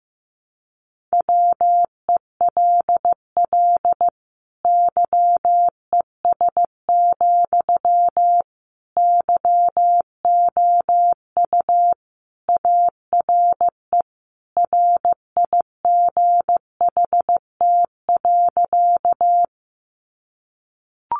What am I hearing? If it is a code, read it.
Morse code, heard as a keyed tone, about 15 wpm, 700 Hz. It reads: WELL YES, YOU ARE RIGHT.
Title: Sentences from Top 500 Words 15wpm